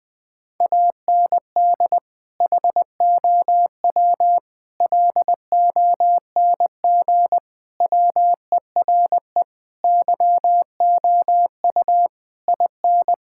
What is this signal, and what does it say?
Morse code, heard as a keyed tone, about 20 wpm, 700 Hz. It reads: AND HOW LONG WERE YOU IN
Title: Sentences from Top 1600 Words 20wpm